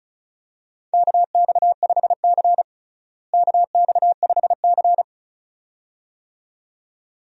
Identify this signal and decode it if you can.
Morse code, heard as a keyed tone, about 36 wpm, 700 Hz. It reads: KX5C KX5C
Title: USA Call Signs - 4 Characters - Repeated 2x - 35wpm